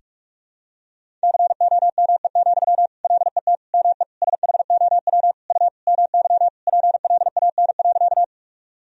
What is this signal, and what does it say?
Morse code, heard as a keyed tone, about 46 wpm, 700 Hz. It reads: COME, LET ME SHOW U MY PLAN.
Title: Sentences from Top 500 Words and Abbreviations 45wpm